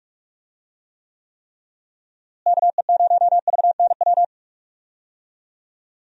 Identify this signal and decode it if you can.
Morse code, heard as a keyed tone, about 46 wpm, 700 Hz. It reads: KE0VNW
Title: USA Call Signs - 6 Characters - Repeated 3x - 45wpm